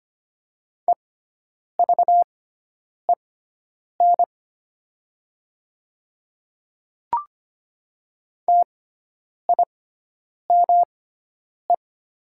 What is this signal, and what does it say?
Morse code, heard as a keyed tone, about 25 wpm, spaced as for 9 wpm, 690 Hz. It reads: EVEN TIME
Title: 6x Character Spacing - Top 200 Words - 25wpm